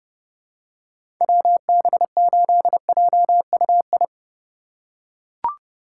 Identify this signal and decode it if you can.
Morse code, heard as a keyed tone, about 30 wpm, 700 Hz. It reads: WB8JUI
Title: K1USN SST QSO Practice - 30wpm